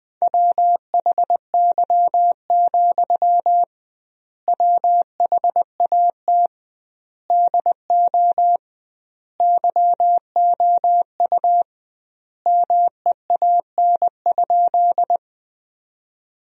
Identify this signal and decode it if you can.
Morse code, heard as a keyed tone, about 20 wpm, 700 Hz. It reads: WHY, WHAT DO YOU MEAN?
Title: 2x Word-Spacing - Sentences from Top 200 Words - 20wpm